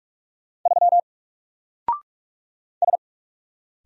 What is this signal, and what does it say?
Morse code, heard as a keyed tone, about 46 wpm, 700 Hz. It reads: 3 S
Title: Single Letter-Number Alt 45wpm